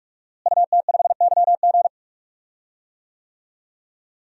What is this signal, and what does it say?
Morse code, heard as a keyed tone, about 43 wpm, 700 Hz. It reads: UT5YG
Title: DXCC-100 Call Signs - 5 Characters - Repeated 3x - 45wpm